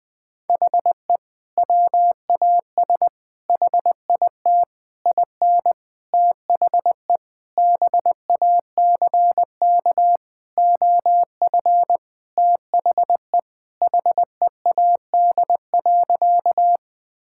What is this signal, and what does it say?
Morse code, heard as a keyed tone, about 21 wpm, 700 Hz. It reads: HE WAS HIT IN THE BACK OF THE HEAD.
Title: Sentences with 6-Word Phrases - 20wpm